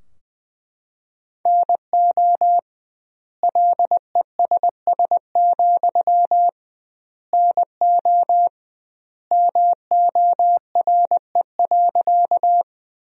Morse code, heard as a keyed tone, about 20 wpm, 700 Hz. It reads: NO LESS, NO MORE.
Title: 2x Word-Spacing - Sentences from Top 400 Words - 20wpm